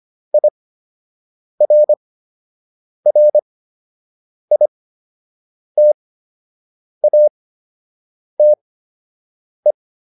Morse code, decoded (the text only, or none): IRRITATE